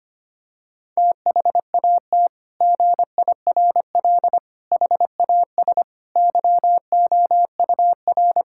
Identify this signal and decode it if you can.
Morse code, heard as a keyed tone, about 25 wpm, 700 Hz. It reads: THAT GIRL HAS YOUR
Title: Sentences from Top 1600 Words 25wpm